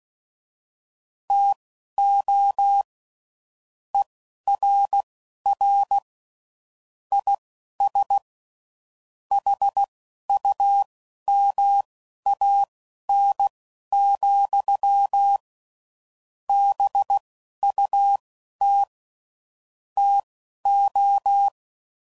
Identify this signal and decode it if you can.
Morse code, heard as a keyed tone, about 16 wpm, spaced as for 11 wpm, 800 Hz. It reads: TO ERR IS HUMAN, BUT TO